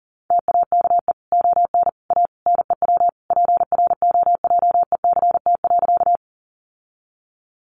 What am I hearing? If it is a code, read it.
Morse code, heard as a keyed tone, about 40 wpm, 700 Hz. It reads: TAKE ON A NEW PROJECT.
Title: Sentences from Top 1400 Words 40wpm